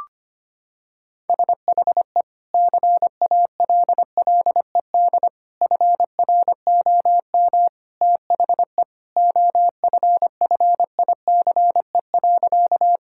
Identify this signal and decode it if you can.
Morse code, heard as a keyed tone, about 25 wpm, 700 Hz. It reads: SHE CALLED FROM THE OFFICE.
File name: practice / 25wpm / Sentences with 2-Word Phrases - 25wpm